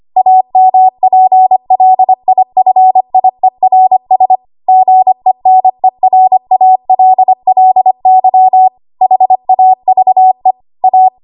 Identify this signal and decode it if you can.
Morse code, heard as a keyed tone, about 25 wpm, 750 Hz. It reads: AMPLIFIERS GENERALLY HAVE A
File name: Archive / 25 WPM / 25WPM